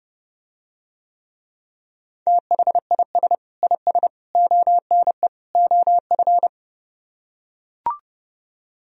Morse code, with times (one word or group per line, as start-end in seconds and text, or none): 2.3-3.4 THIS
3.6-4.1 IS
4.4-5.3 ONE
5.5-6.5 OF